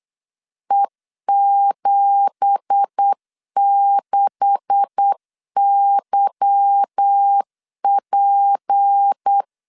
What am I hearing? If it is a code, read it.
Morse code, heard as a keyed tone, about 8 wpm, 800 Hz. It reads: E76YP
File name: LEZIONI / 9 2867 8wpm 3440